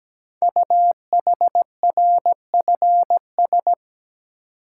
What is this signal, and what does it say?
Morse code, heard as a keyed tone, about 17 wpm, 700 Hz. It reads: UHRFS
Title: Code Groups - 5 Letters - 17wpm